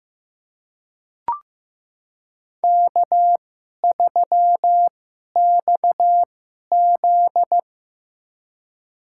Morse code, heard as a keyed tone, about 15 wpm, spaced as for 11 wpm, 700 Hz. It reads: K3XZ
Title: USA Call Signs - 4 Characters - 2x Character Spacing - 15wpm